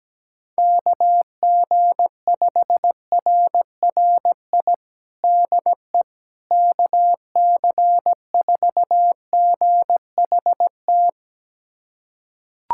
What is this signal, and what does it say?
Morse code, heard as a keyed tone, about 17 wpm, 700 Hz. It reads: KG5RRI DE KC4GHT
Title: USA Callsign DE Callsign - 6 Characters - 17wpm